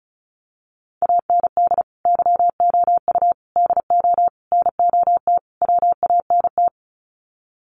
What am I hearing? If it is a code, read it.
Morse code, heard as a keyed tone, about 34 wpm, 700 Hz. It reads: AND YOU DO NOT WANT
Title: Sets of 5 Words Encore2 35wpm